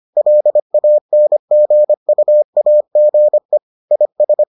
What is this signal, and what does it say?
Morse code, heard as a keyed tone, about 25 wpm, 600 Hz. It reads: LANGUAGE IS